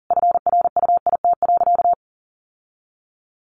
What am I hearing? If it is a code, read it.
Morse code, heard as a keyed tone, about 40 wpm, 700 Hz. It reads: FRUIT.
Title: Sentences from Top 1300 Words 40wpm